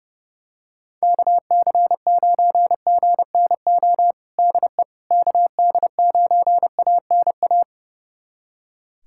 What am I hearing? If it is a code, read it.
Morse code, heard as a keyed tone, about 32 wpm, 700 Hz. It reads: KC9GNO DE KD9ANA
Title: USA Callsign DE Callsign - 6 Characters - 30wpm